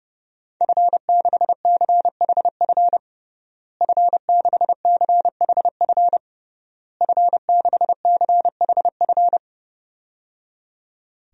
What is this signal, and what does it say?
Morse code, heard as a keyed tone, about 30 wpm, 700 Hz. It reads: F6CHF F6CHF F6CHF
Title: DXCC-100 Call Signs - 5 Characters - Repeated 3x - 30wpm